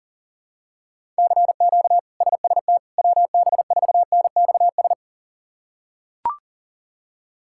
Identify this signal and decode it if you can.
Morse code, heard as a keyed tone, about 40 wpm, 700 Hz. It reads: CQ SST WB4NXS
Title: K1USN SST QSO Practice - 40wpm